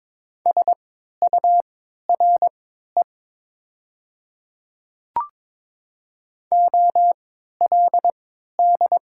Morse code, heard as a keyed tone, about 22 wpm, spaced as for 12 wpm, 700 Hz. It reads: SURE OLD
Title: 3x Character Spacing - Top 300 Words - 22wpm